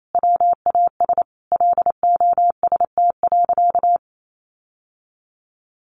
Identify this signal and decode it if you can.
Morse code, heard as a keyed tone, about 28 wpm, 700 Hz. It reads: WAS LOST.
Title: Sentences from Top 1400 Words 28wpm